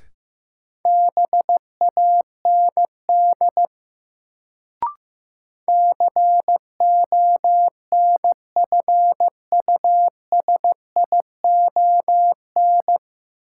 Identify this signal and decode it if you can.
Morse code, heard as a keyed tone, about 16 wpm, 700 Hz. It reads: BAND CONFUSION
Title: Sentences from Top 1600 Words - New Word Drill - 15wpm